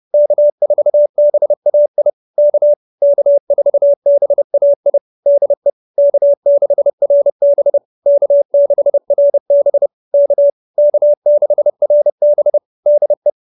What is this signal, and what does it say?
Morse code, heard as a keyed tone, about 30 wpm, 590 Hz. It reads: K4BAI K K4BAI DE K6RB K6RB K K6RB DE